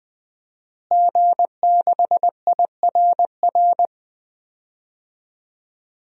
Morse code, heard as a keyed tone, about 20 wpm, 700 Hz. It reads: G6IRR
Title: DXCC-100 Call Signs - 5 Characters - Repeated 3x - 20wpm